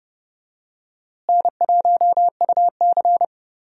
Morse code, heard as a keyed tone, about 30 wpm, 700 Hz. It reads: N1UC
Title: USA Call Signs - 4 Characters - Repeated 2x - 30wpm